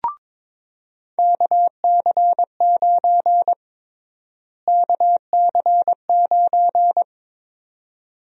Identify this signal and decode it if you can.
Morse code, heard as a keyed tone, about 22 wpm, 700 Hz. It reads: KC9 KC9
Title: USA Call Sign Prefixes - Repeated 2x - 22wpm